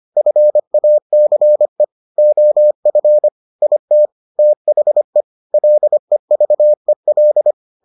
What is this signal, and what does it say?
Morse code, heard as a keyed tone, about 25 wpm, 600 Hz. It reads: FACE OF IT THE LEVEL